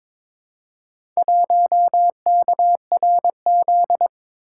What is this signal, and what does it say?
Morse code, heard as a keyed tone, about 21 wpm, 700 Hz. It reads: 1KRZ
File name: practice / 22wpm / Four Letter-Number Combinations - 22wpm